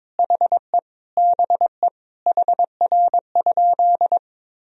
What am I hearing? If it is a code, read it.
Morse code, heard as a keyed tone, about 21 wpm, 700 Hz. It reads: HE BE HR?
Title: Sentences from Top 400 Words and Abbreviations 22wpm